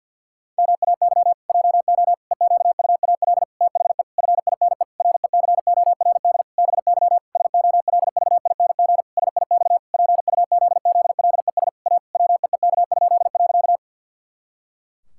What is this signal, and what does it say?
Morse code, heard as a keyed tone, about 47 wpm, 700 Hz. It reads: MAY JO EQUAL THE FINE RECORD BY SOLVING SIX PUZZLES A WEEK?.
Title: Pangram Challenge v2 - 50wpm